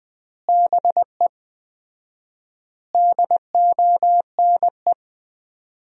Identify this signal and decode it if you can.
Morse code, heard as a keyed tone, about 20 wpm, 700 Hz. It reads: BE DONE
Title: 4x Word-Spacing - Sentences from Top 500 Words - 20wpm